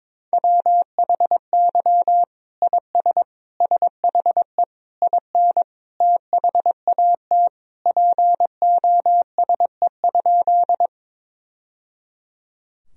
Morse code, heard as a keyed tone, about 22 wpm, 700 Hz. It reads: WHY IS SHE IN THAT POSE?
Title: Sentences from Top 1000 Words - Review Part 2 - 22wpm